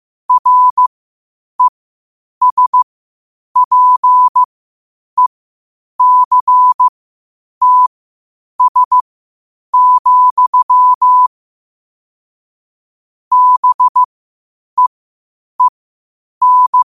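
Morse code, heard as a keyed tone, about 15 wpm, spaced as for 8 wpm, 1000 Hz. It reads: RESPECTS, BEEN
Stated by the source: Morse code quotes at 8 words per minute from QOTD at 8 WPM.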